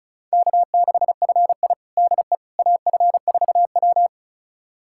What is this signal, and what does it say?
Morse code, heard as a keyed tone, about 36 wpm, 700 Hz. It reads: K6FI DE AF4W
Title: USA Callsign DE Callsign - 4 Characters - 35wpm